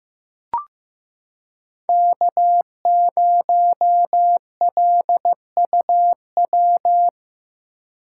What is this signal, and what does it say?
Morse code, heard as a keyed tone, about 15 wpm, 700 Hz. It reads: K0LUW
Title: K1USN SST QSO Practice - 15wpm